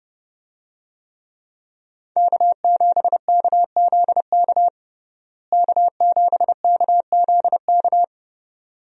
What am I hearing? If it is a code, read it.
Morse code, heard as a keyed tone, about 30 wpm, 700 Hz. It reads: K7KZK K7KZK
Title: USA Call Signs - 5 Characters - Repeated 2x - 30wpm